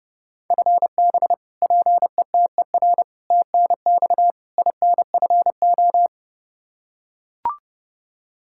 Morse code, heard as a keyed tone, about 30 wpm, 700 Hz. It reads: FB PETER TNX INFO
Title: Practice QSO - Exchange 3 - More Info - 30wpm